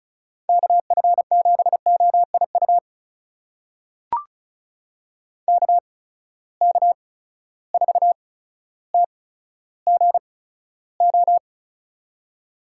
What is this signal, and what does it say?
Morse code, heard as a keyed tone, about 34 wpm, spaced as for 15 wpm, 700 Hz. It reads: KF7OIU KK4TGO
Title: USA Call Signs - 6 Characters - 8x Character Spacing - 35wpm